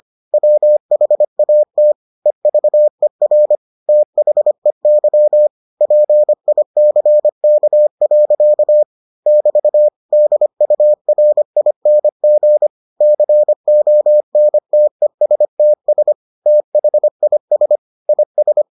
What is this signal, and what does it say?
Morse code, heard as a keyed tone, about 25 wpm, 600 Hz. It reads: WHAT EVER THEY PICK. <BT> DURING CONTESTS THIS IS